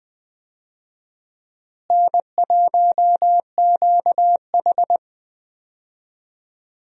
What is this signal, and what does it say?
Morse code, heard as a keyed tone, about 21 wpm, 700 Hz. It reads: N1QH
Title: USA Call Signs - 4 Characters - Repeated 3x - 20wpm